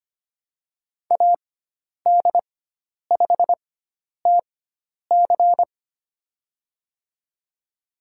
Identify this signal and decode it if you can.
Morse code, heard as a keyed tone, about 25 wpm, spaced as for 10 wpm, 700 Hz. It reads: AD5TC
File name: Lesson 16g - Letter c - Call Signs 25wpm